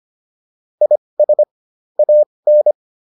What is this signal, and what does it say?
Morse code, heard as a keyed tone, about 25 wpm, spaced as for 20 wpm, 600 Hz. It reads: IS AN
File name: WD102 20